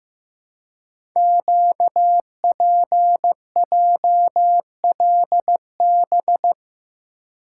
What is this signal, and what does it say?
Morse code, heard as a keyed tone, about 15 wpm, 700 Hz. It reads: QPJLB
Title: Code Groups - 5 Letters - 15wpm